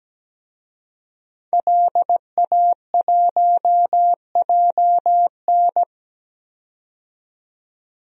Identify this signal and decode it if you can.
Morse code, heard as a keyed tone, about 17 wpm, 700 Hz. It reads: LA1JN
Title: DXCC-100 Call Signs - 5 Characters - 17wpm